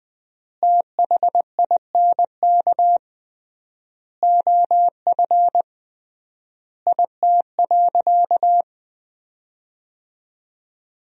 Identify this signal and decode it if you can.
Morse code, heard as a keyed tone, about 21 wpm, 700 Hz. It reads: THINK OF IT.
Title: 3x Word-Spacing - Sentences from Top 200 Words - 20wpm